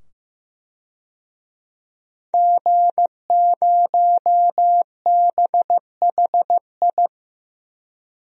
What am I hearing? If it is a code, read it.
Morse code, heard as a keyed tone, about 15 wpm, 700 Hz. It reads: G0BHI
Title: DXCC-100 Call Signs - 5 Characters - 15wpm